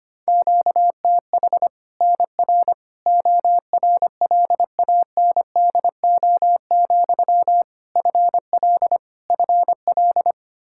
Morse code, heard as a keyed tone, about 25 wpm, 700 Hz. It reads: QTH NR ORLANDO, FL FL
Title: Practice QSO - Exchange 1 - Beginning - 25wpm